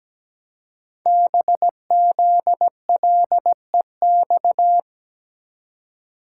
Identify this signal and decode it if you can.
Morse code, heard as a keyed tone, about 17 wpm, 700 Hz. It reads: BZLEX
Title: Code Groups - 5 Letters - 17wpm